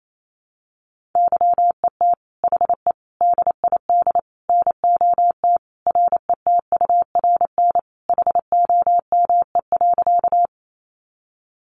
Morse code, heard as a keyed tone, about 28 wpm, 700 Hz. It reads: YET HE DID NOT RETURN HOME.
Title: Sentences from Top 800 Words 28wpm